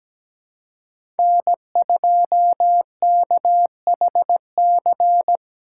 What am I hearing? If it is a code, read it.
Morse code, heard as a keyed tone, about 17 wpm, 700 Hz. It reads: N2KHC
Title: USA Call Signs - 5 Characters - 2x Character Spacing - 17wpm